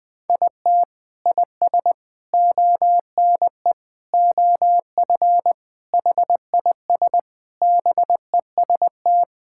Morse code, heard as a keyed tone, about 20 wpm, 700 Hz. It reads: IT IS ONE OF HIS BEST
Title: Sentences with 3-Word Phrases - Spoken Beginning - 20wpm